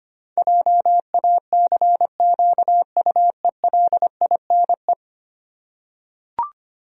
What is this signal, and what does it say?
Morse code, heard as a keyed tone, about 25 wpm, 700 Hz. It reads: JACQUELINE
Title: USA Common Names - 25wpm